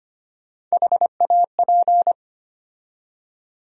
Morse code, heard as a keyed tone, about 25 wpm, 700 Hz. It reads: HAP
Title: USA Call Sign Postfixes - 3 Letters 25wpm